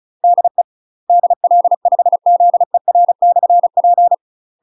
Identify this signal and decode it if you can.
Morse code, heard as a keyed tone, about 34 wpm, 700 Hz. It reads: DE DL5ZER/P
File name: UK MASTER QSOs TRAINING / 35WPM / UK MASTER QSOs 0040